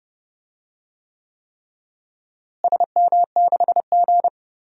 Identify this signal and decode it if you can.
Morse code, heard as a keyed tone, about 30 wpm, 700 Hz. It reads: SM6G
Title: DXCC-100 Call Signs - 4 Characters - Repeated 3x - 30wpm